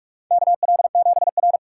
Morse code, heard as a keyed tone, about 44 wpm, 690 Hz. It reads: KL7R
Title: USA Call Signs - 4 Characters - 3x Character Spacing - 45wpm